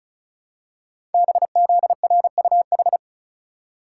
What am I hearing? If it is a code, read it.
Morse code, heard as a keyed tone, about 36 wpm, 690 Hz. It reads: BZRUH
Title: Code Groups - 5 Letters - 35wpm